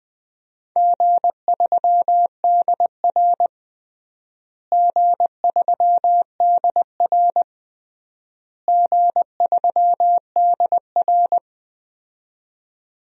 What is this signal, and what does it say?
Morse code, heard as a keyed tone, about 20 wpm, 700 Hz. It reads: G3DR G3DR G3DR
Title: DXCC-100 Call Signs - 4 Characters - Repeated 3x - 20wpm